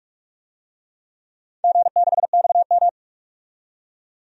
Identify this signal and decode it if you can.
Morse code, heard as a keyed tone, about 46 wpm, 700 Hz. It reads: G6XM
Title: DXCC-100 Call Signs - 4 Characters - 45wpm